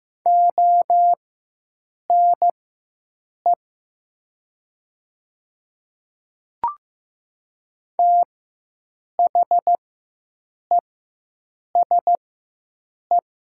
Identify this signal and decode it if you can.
Morse code, heard as a keyed tone, about 15 wpm, spaced as for 7 wpm, 700 Hz. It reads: ONE THESE
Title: 4x Character Spacing - Top 100 Words - 15wpm